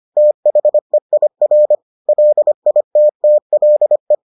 Morse code, heard as a keyed tone, about 25 wpm, 600 Hz. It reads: THEIR LITTLE